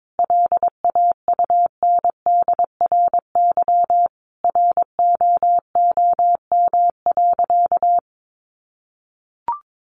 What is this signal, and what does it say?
Morse code, heard as a keyed tone, about 22 wpm, 700 Hz. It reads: LAUNDRY ROOM.
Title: Easy Phrases 22wpm